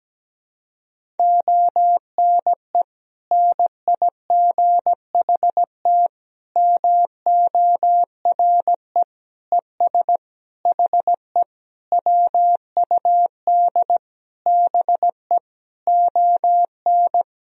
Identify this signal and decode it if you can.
Morse code, heard as a keyed tone, about 17 wpm, 700 Hz. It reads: ONE NIGHT MORE ES HE WUD BE ON
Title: Sentences from Top 300 Words and Abbreviations 17wpm